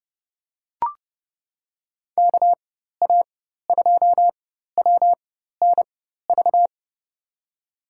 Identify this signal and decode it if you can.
Morse code, heard as a keyed tone, about 30 wpm, spaced as for 14 wpm, 700 Hz. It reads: KA2WNV
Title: USA Call Signs - 6 Characters - 4x Character Spacing - 30wpm